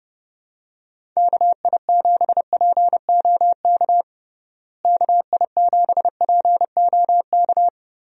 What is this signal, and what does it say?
Morse code, heard as a keyed tone, about 32 wpm, 700 Hz. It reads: KI7POK KI7POK
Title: USA Call Signs - 6 Characters - Repeated 3x - 30wpm